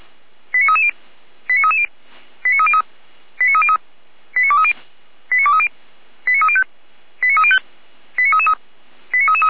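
Five-tone selective call (ZVEI), heard as quick runs of five tones, tones of about 70 ms. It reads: 89330 89330 89303 89303 89230 89230 89305 89305 89333
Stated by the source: ZVEI five-tone call received on SDR, logged by Feuerwehr Orlamünde.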